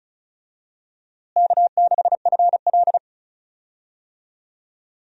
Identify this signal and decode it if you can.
Morse code, heard as a keyed tone, about 36 wpm, 700 Hz. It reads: K6FL
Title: USA Call Signs - 4 Characters - Repeated 3x - 35wpm